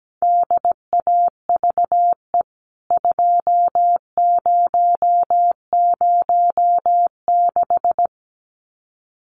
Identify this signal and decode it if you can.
Morse code, heard as a keyed tone, about 17 wpm, 700 Hz. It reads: DAVE 2006
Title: CWT Exchange 17wpm